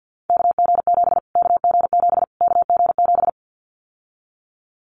Morse code, heard as a keyed tone, about 50 wpm, 700 Hz. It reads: KG7 KG7 KG7
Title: Callsign Prefixes for Top 25 DXCC - Repeated 3x - 50wpm